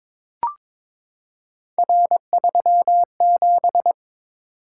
Morse code, heard as a keyed tone, about 22 wpm, 700 Hz. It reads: R37
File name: practice / 22wpm / Three Letter-Number Combinations - 22wpm